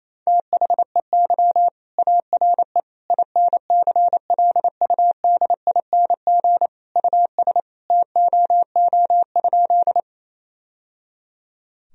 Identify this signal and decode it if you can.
Morse code, heard as a keyed tone, about 28 wpm, 700 Hz. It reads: THEY ARE INCLUDING US TOO?
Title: Sentences from Top 1000 Words - Review Part 4 - 28wpm